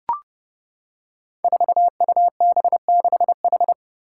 Morse code, heard as a keyed tone, about 30 wpm, 700 Hz. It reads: 4UB6H
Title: Five Challenging Letter-Number Combinations 30wpm